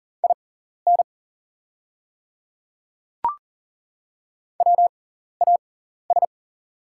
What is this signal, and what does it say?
Morse code, heard as a keyed tone, about 42 wpm, spaced as for 14 wpm, 700 Hz. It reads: IN WAS
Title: 6x Character Spacing - Top 100 Words - 40wpm